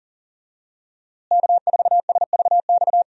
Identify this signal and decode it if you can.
Morse code, heard as a keyed tone, about 39 wpm, 700 Hz. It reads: K4SVX
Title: USA Call Signs - 5 Characters - Repeated 3x - 40wpm